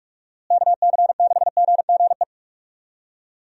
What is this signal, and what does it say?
Morse code, heard as a keyed tone, about 46 wpm, 700 Hz. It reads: KC6GGE
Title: USA Call Signs - 6 Characters - Repeated 3x - 45wpm